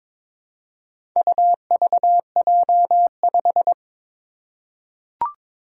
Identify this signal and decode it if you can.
Morse code, heard as a keyed tone, about 22 wpm, 700 Hz. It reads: UVJ5
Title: Four Challenging Letter-Number Combinations 22wpm